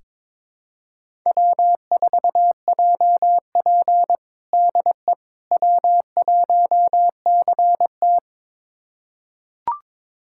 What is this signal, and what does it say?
Morse code, heard as a keyed tone, about 22 wpm, 700 Hz. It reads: W4JP DE W1CT
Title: USA Callsign DE Callsign - 4 Characters - 22wpm